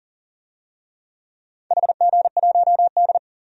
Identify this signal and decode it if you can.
Morse code, heard as a keyed tone, about 40 wpm, 700 Hz. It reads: HG1D
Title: DXCC-100 Call Signs - 4 Characters - 40wpm